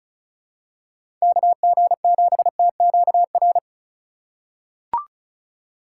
Morse code, heard as a keyed tone, about 36 wpm, 700 Hz. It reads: KG7TQR
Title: USA Call Signs - 2x Character Spacing - 35wpm